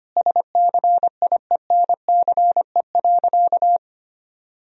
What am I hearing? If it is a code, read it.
Morse code, heard as a keyed tone, about 25 wpm, 700 Hz. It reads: SCIENCE.
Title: Sentences from Top 1000 Words - Review Part 2 - 25wpm